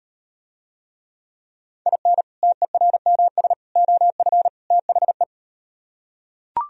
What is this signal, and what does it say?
Morse code, heard as a keyed tone, about 38 wpm, 700 Hz. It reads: IN TERMS OF THE